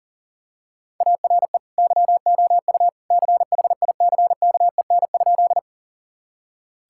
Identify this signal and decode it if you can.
Morse code, heard as a keyed tone, about 38 wpm, 700 Hz. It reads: ARE YOU CHICKEN?
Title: Sentences from Top 1000 Words - Review Part 5 - 40wpm